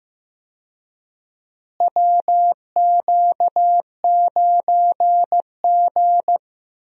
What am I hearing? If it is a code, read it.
Morse code, heard as a keyed tone, about 15 wpm, 700 Hz. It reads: WQ9G
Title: USA Call Signs - 4 Characters - Repeated 3x - 15wpm